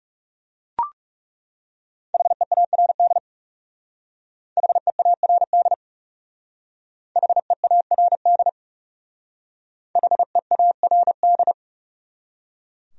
Morse code, heard as a keyed tone, about 37 wpm, 700 Hz. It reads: HEARD HEARD HEARD HEARD
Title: Speed-Racing Top 500 Words 30wpm